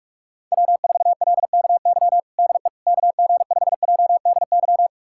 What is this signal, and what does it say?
Morse code, heard as a keyed tone, about 47 wpm, 700 Hz. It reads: W4LKY DE KG5JDY
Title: USA Callsign DE Callsign 45wpm